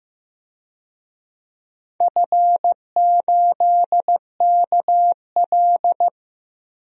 Morse code, heard as a keyed tone, about 15 wpm, 700 Hz. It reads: F8KL
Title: DXCC-100 Call Signs - 4 Characters - Repeated 3x - 15wpm